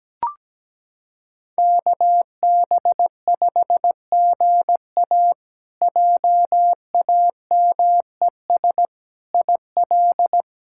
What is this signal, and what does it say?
Morse code, heard as a keyed tone, about 17 wpm, 700 Hz. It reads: KB5GA JAMES IL
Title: K1USN SST QSO Practice - 17wpm